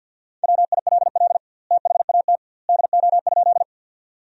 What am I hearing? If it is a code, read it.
Morse code, heard as a keyed tone, about 50 wpm, 700 Hz. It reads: WILL THAT DO?
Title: 2x Word-Spacing - Sentences from Top 100 Words - 50wpm